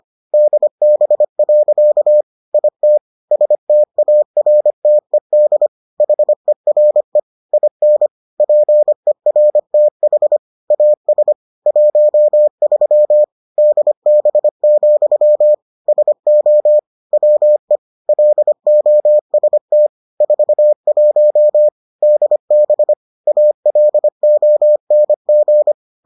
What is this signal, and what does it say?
Morse code, heard as a keyed tone, about 25 wpm, 600 Hz. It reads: DB. IT STARTED HERE IN PERTH AS 13 DB, SO WE LOST 41 DB ALONG